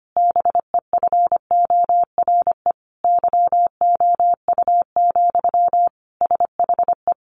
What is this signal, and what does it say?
Morse code, heard as a keyed tone, about 25 wpm, 700 Hz. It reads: BEFORE YOU, SHE
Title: Sentences from Top 400 Words 25wpm